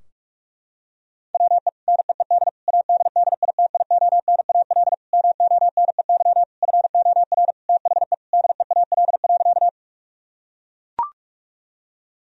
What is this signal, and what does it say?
Morse code, heard as a keyed tone, about 41 wpm, 700 Hz. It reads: WE NEED ADDITIONAL MONEY FOR THE DEAL.
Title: Sentences from Top 1000 Words - Review Part 4 - 45wpm